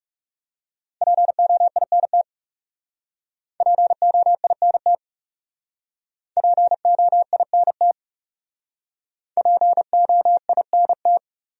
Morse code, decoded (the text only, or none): POINT POINT POINT POINT